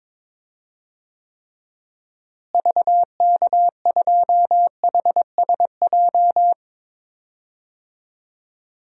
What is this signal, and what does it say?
Morse code, heard as a keyed tone, about 22 wpm, 700 Hz. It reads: VK2HSJ
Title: DXCC-100 Call Signs - 6 Characters - Repeated 3x - 22wpm